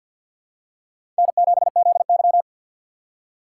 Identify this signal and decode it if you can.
Morse code, heard as a keyed tone, about 52 wpm, 700 Hz. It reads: N7ZY
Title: USA Call Signs - 4x Character Spacing - 50wpm